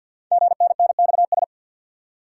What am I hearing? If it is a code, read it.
Morse code, heard as a keyed tone, about 50 wpm, 700 Hz. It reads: GNNXS
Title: Code Groups - 5 Letters - 50wpm